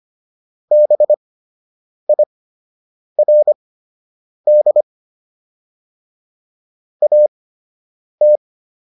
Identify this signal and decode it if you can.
Morse code, heard as a keyed tone, about 25 wpm, spaced as for 8 wpm, 600 Hz. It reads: BIRD AT